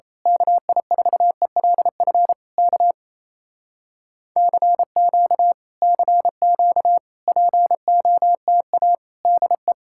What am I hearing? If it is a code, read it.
Morse code, heard as a keyed tone, about 27 wpm, 700 Hz. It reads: KI4ELF K CQ CQ POTA DE
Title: Speed-Racing Common QSO Phrases Encore - 28wpm